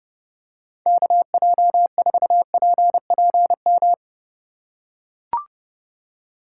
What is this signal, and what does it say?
Morse code, heard as a keyed tone, about 29 wpm, 700 Hz. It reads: KJ4PPM